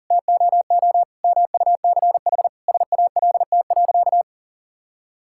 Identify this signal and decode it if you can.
Morse code, heard as a keyed tone, about 40 wpm, 700 Hz. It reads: TOO MUCH SALT.
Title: Sentences from Top 1000 Words - Review Part 5 - 40wpm